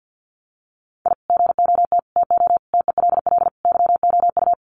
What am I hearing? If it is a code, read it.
Morse code, heard as a keyed tone, about 50 wpm, 700 Hz. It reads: I GOT TO TELL YOU
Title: Sets of 5 Words Encore2 50wpm